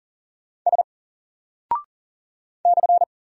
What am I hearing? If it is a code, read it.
Morse code, heard as a keyed tone, about 40 wpm, 700 Hz. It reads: S /
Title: Single Character (Letter:Number:Punctuation:Prosign) 40wpm